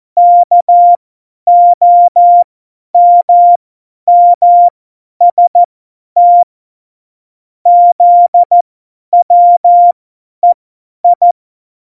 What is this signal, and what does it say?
Morse code, heard as a keyed tone, about 14 wpm, spaced as for 10 wpm, 700 Hz. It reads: KOMMST ZWEI
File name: Morsekurs / Witze / Witz 14. Mai 2019, 50 F70